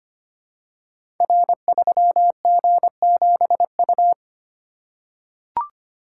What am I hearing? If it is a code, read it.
Morse code, heard as a keyed tone, about 25 wpm, 690 Hz. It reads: R3G7U
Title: Code Groups - 5 Letter:Numbers - 25wpm